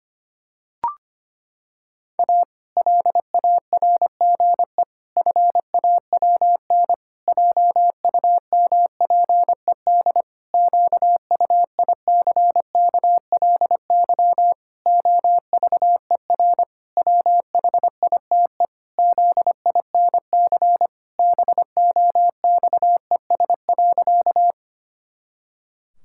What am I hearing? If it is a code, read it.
Morse code, heard as a keyed tone, about 25 wpm, 700 Hz. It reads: A LARGE FAWN JUMPED QUICKLY OVER WHITE ZINC BOXES.
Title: Pangram Challenge v2 - 25wpm